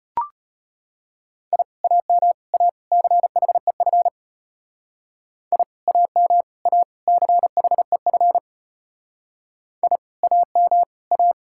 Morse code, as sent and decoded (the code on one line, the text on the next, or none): .. / .- -- / .- / -.-. .... ..-. / .. / .- -- / .- / -.-. .... ..-. / .. / .- -- / .-
I AM A CHEF I AM A CHEF I AM A